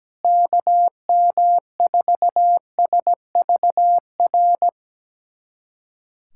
Morse code, heard as a keyed tone, about 17 wpm, 700 Hz. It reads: KM4SVR
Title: USA Callsign DE Callsign - 6 Characters - 17wpm